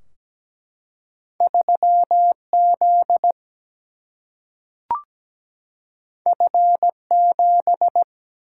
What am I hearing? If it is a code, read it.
Morse code, heard as a keyed tone, about 17 wpm, 700 Hz. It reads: 3Z F7
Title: Single-Letter Single-Number 17wpm